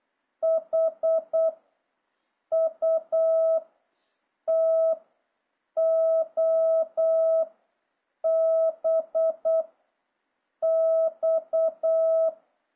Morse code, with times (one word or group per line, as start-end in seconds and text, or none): 0.4-1.5 H
2.5-3.6 U
4.5-4.9 T
5.8-7.4 O
8.2-9.6 B
10.6-12.3 X